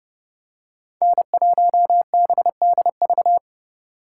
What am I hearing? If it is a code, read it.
Morse code, heard as a keyed tone, about 29 wpm, 700 Hz. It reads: N1BDV